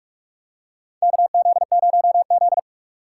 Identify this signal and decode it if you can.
Morse code, heard as a keyed tone, about 43 wpm, 700 Hz. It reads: KZ0Z
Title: USA Call Signs - 4 Characters - 4x Character Spacing - 45wpm